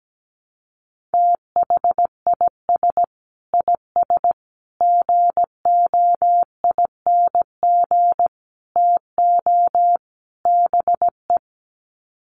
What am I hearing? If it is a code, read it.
Morse code, heard as a keyed tone, about 17 wpm, 700 Hz. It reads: THIS IS GOING TO BE
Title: Sets of 5 Words 17wpm